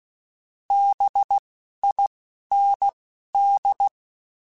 Morse code, heard as a keyed tone, about 16 wpm, 800 Hz. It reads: B I N D